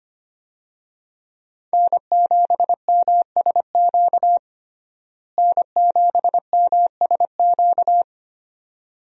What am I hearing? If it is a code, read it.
Morse code, heard as a keyed tone, about 25 wpm, 700 Hz. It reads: N7MSQ N7MSQ
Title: USA Call Signs - 5 Characters - Repeated 3x - 25wpm